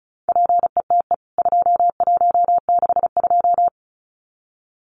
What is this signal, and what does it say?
Morse code, heard as a keyed tone, about 34 wpm, 700 Hz. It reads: PETE 2162
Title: CWT Exchange 35wpm